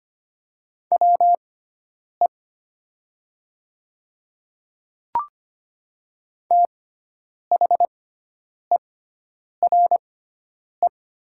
Morse code, heard as a keyed tone, about 25 wpm, spaced as for 9 wpm, 700 Hz. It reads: WE THERE
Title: 6x Character Spacing - Top 200 Words - 25wpm